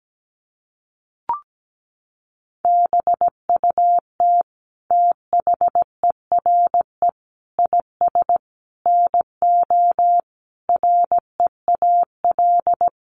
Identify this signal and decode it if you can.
Morse code, heard as a keyed tone, about 17 wpm, 700 Hz. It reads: BUT THERE IS NO REAL
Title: Sentences from Top 1300 Words 17wpm